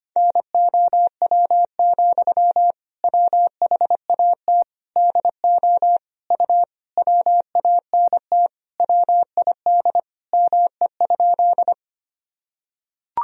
Morse code, heard as a keyed tone, about 25 wpm, 700 Hz. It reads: NOW, WHAT DO U WANT WID ME?
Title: Sentences from Top 200 Words and Abbreviations 25wpm